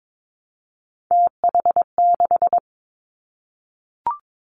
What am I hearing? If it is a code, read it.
Morse code, heard as a keyed tone, about 22 wpm, 700 Hz. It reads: TH6
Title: Three Challenging Letter-Number Combinations 22wpm